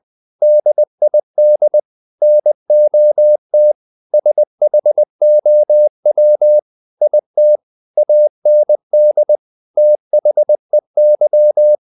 Morse code, heard as a keyed tone, about 20 wpm, 600 Hz. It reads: DID NOT SHOW IT AND THEY
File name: the old man and the sea / the old man and the sea part 1 20wpm